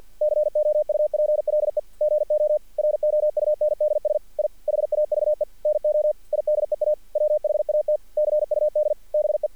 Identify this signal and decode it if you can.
Morse code, heard as a keyed tone, about 48 wpm, 600 Hz. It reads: COUPLE GO ROUNDS I HAVE NO IDEA WHAT CUD BE
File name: archived / CRSnet